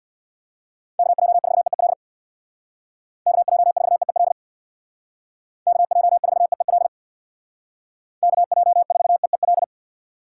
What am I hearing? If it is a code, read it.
Morse code, heard as a keyed tone, about 62 wpm, 700 Hz. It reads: KJ4EEL KJ4EEL KJ4EEL KJ4EEL
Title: Speed Racing - USA 2x3 Call Signs 50wpm